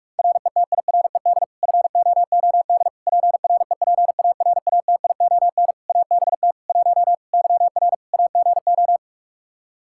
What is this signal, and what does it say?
Morse code, heard as a keyed tone, about 46 wpm, 700 Hz. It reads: RETIRED FOOD PREPARATION ABT 1 YR AGO
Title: Practice QSO - Exchange 3 - More Info - 45wpm